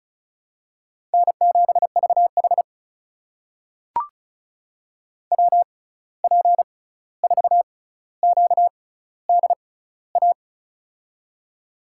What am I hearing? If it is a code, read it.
Morse code, heard as a keyed tone, about 36 wpm, spaced as for 16 wpm, 700 Hz. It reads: N7VH WP4QDA
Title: USA Call Signs - 6x Character Spacing - 35wpm